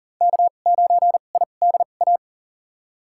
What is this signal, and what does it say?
Morse code, heard as a keyed tone, about 40 wpm, spaced as for 29 wpm, 700 Hz. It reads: K9IDA
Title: USA Call Signs - 5 Characters - 2x Character Spacing - 40wpm